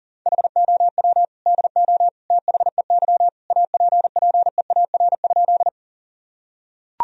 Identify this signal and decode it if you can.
Morse code, heard as a keyed tone, about 40 wpm, 700 Hz. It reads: HOW DO THEY APPEAR?
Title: Sentences from Top 1000 Words - Review Part 2 - 40wpm